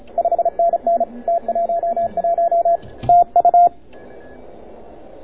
Morse code, heard as a keyed tone, about 27 wpm, 690 Hz. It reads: 5NN T11 TU